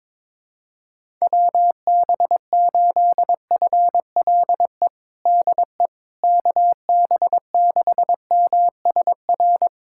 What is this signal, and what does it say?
Morse code, heard as a keyed tone, about 22 wpm, 700 Hz. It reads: WB8FLE DE KB6MSR
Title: USA Callsign DE Callsign - 6 Characters - 22wpm